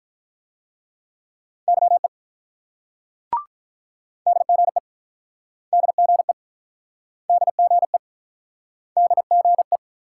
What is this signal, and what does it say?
Morse code, heard as a keyed tone, about 50 wpm, 700 Hz. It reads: YE DGE DGE DGE DGE